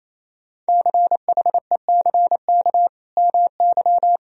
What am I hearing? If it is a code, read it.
Morse code, heard as a keyed tone, about 28 wpm, 700 Hz. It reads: CHECK MY
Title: Sentences from Top 1000 Words - Review Part 3 - 28wpm